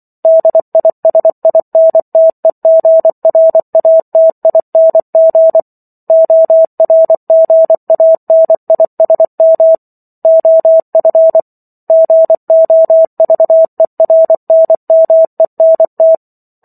Morse code, heard as a keyed tone, about 24 wpm, 650 Hz. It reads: DISINTEGRATING ORGANISM OF GOVERNMENT